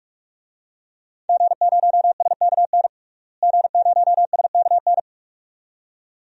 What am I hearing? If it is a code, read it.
Morse code, heard as a keyed tone, about 41 wpm, 700 Hz. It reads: G0SKN G0SKN
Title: DXCC-100 Call Signs - 5 Characters - Repeated 2x - 45wpm